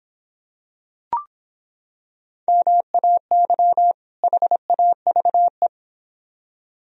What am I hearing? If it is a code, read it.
Morse code, heard as a keyed tone, about 25 wpm, 700 Hz. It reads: MAY HAVE